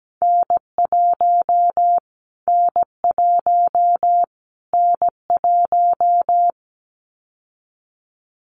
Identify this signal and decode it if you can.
Morse code, heard as a keyed tone, about 17 wpm, 700 Hz. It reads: N1 N1 N1
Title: Callsign Prefixes for Top 25 DXCC - Repeated 3x - 17wpm